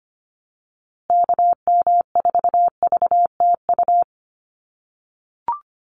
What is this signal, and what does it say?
Morse code, heard as a keyed tone, about 25 wpm, 700 Hz. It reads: KM4VTU
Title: USA Call Signs Repeated 2x 25wpm